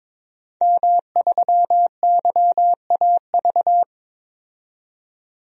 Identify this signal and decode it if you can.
Morse code, heard as a keyed tone, about 21 wpm, 700 Hz. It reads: M3YAV
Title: DXCC-100 Call Signs - 5 Characters - 22wpm